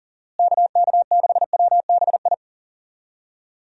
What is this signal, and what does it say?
Morse code, heard as a keyed tone, about 37 wpm, 700 Hz. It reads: KK6WBI
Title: USA Call Signs - 6 Characters - Repeated 3x - 40wpm